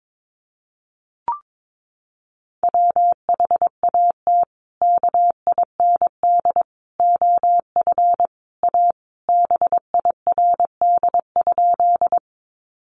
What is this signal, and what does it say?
Morse code, heard as a keyed tone, about 22 wpm, 700 Hz. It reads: WHAT KIND OF A BIRD?
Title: Sentences from Top 400 Words 22wpm